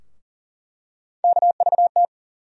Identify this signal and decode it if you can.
Morse code, heard as a keyed tone, about 40 wpm, 700 Hz. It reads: KVT
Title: USA Call Sign Postfixes - 3 Letters 40wpm